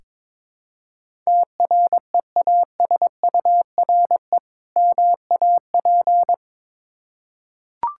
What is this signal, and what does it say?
Morse code, heard as a keyed tone, about 22 wpm, 690 Hz. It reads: TREASURE MAP
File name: Difficult Words with Hints - 22wpm